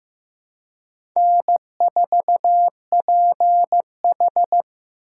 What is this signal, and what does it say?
Morse code, heard as a keyed tone, about 15 wpm, 700 Hz. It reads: N4PH
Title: USA Call Signs - 4 Characters - 4x Character Spacing - 15wpm